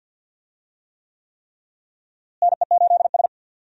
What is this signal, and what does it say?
Morse code, heard as a keyed tone, about 51 wpm, 700 Hz. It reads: NE8S
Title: USA Call Signs - 4 Characters - Repeated 3x - 50wpm